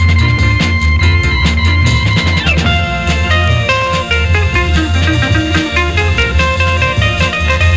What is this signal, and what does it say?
Dial-up modem handshake.